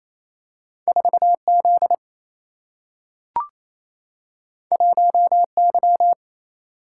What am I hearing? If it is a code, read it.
Morse code, heard as a keyed tone, about 28 wpm, 700 Hz. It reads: 4 Z 1 Y
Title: Single-Letter Single-Number 28wpm